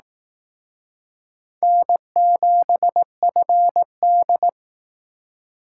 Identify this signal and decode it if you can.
Morse code, heard as a keyed tone, about 19 wpm, 700 Hz. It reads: N7FD